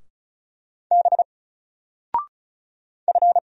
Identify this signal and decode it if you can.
Morse code, heard as a keyed tone, about 36 wpm, 700 Hz. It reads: B F
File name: Single-Letter-Number-35wpm